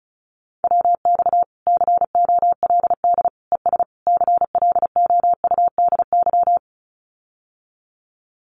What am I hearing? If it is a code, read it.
Morse code, heard as a keyed tone, about 34 wpm, 700 Hz. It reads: WX COLD ES CLOUDY
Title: Common QSO Phrases Encore 35wpm